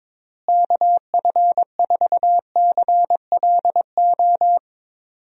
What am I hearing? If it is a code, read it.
Morse code, heard as a keyed tone, about 22 wpm, 700 Hz. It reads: KF4CLO
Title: USA Call Signs - 6 Characters - Repeated 3x - 22wpm